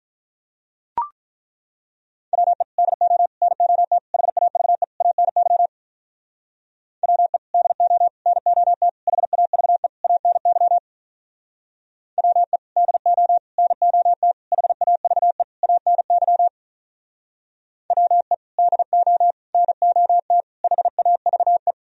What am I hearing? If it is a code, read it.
Morse code, heard as a keyed tone, about 43 wpm, 700 Hz. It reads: WE DO NOT HAVE ANY WE DO NOT HAVE ANY WE DO NOT HAVE ANY WE DO NOT HAVE
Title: Speed-Racing - Sets of 5 Words 35wpm